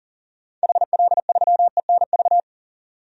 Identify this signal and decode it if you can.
Morse code, heard as a keyed tone, about 40 wpm, 700 Hz. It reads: HL3ENV
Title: DXCC-100 Call Signs - 6 Characters - 40wpm